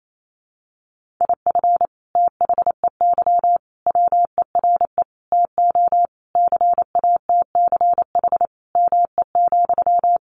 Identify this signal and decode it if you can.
Morse code, heard as a keyed tone, about 28 wpm, 700 Hz. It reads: IF THEY WERE TO CATCH ME,
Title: Sentences from Top 1500 Words 28wpm